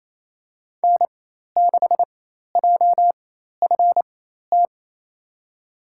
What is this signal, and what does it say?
Morse code, heard as a keyed tone, about 28 wpm, spaced as for 13 wpm, 700 Hz. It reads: N6JFT
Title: USA Call Signs - 4x Character Spacing - 28wpm